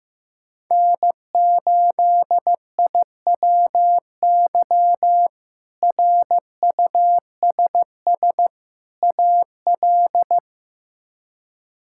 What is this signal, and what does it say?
Morse code, heard as a keyed tone, about 16 wpm, 700 Hz. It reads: N8IWY RUSS AL
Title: K1USN SST QSO Practice - 15wpm